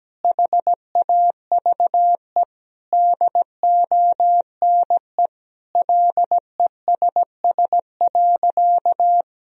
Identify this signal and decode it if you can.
Morse code, heard as a keyed tone, about 17 wpm, 700 Hz. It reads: HAVE DONE LESS.
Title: Sentences from Top 1000 Words - Review Part 3 - 17wpm